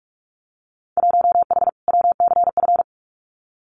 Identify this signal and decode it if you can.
Morse code, heard as a keyed tone, about 44 wpm, 700 Hz. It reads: 1H WCF
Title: Winter Field Day Exchanges 45wpm